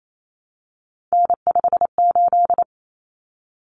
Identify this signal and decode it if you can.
Morse code, heard as a keyed tone, about 28 wpm, 700 Hz. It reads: N58
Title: Common Cut Numbers - 3-Digit 28wpm